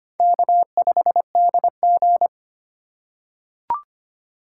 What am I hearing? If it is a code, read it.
Morse code, heard as a keyed tone, about 25 wpm, 700 Hz. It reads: K5DG
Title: USA Call Signs - 4 Characters - 3x Character Spacing - 25wpm